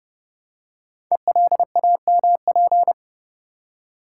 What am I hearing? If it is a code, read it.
Morse code, heard as a keyed tone, about 30 wpm, 700 Hz. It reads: ELAMP
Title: Code Groups - 5 Letters - 30wpm